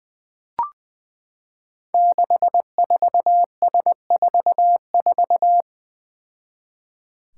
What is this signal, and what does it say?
Morse code, heard as a keyed tone, about 20 wpm, 700 Hz. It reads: 64S44
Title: Five Challenging Letter-Number Combinations 20wpm